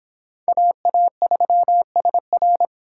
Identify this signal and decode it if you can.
Morse code, heard as a keyed tone, about 26 wpm, 700 Hz. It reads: AA3SR